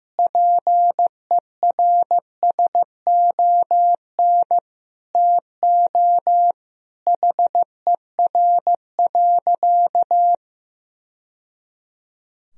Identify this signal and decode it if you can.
Morse code, heard as a keyed tone, about 16 wpm, 700 Hz. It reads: PERSON TO HER.
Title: Sentences from Top 1000 Words - Review Part 3 - 15wpm